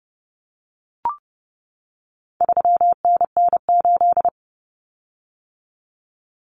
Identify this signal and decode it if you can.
Morse code, heard as a keyed tone, about 30 wpm, 700 Hz. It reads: 3NN8
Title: Common Cut Numbers - 4-Digit 30wpm